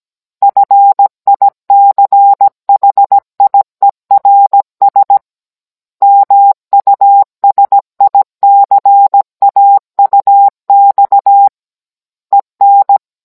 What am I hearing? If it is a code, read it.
Morse code, heard as a keyed tone, about 17 wpm, 800 Hz. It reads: FICHIERS MUSICAUX EN